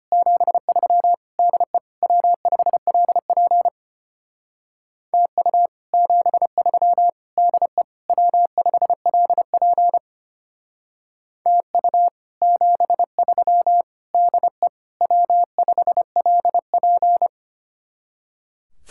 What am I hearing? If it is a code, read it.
Morse code, heard as a keyed tone, about 32 wpm, 700 Hz. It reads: 73 DE W5LP TU 73 DE W5LP TU 73 DE W5LP